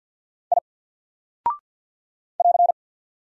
Morse code, heard as a keyed tone, about 48 wpm, 700 Hz. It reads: I <AR>
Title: Single Character (Letter:Number:Punctuation:Prosign) Alt 50wpm